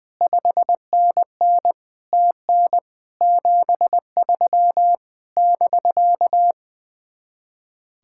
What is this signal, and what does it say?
Morse code, heard as a keyed tone, about 20 wpm, 700 Hz. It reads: 5NN TN 73 <BK>
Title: Speed-Racing Common QSO Phrases - 20wpm